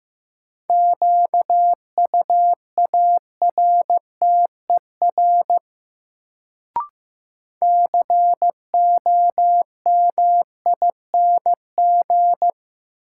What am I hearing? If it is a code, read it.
Morse code, heard as a keyed tone, about 15 wpm, 700 Hz. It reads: QUARTER COMING
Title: Sentences from Top 1300 Words - New Word Drill - 15wpm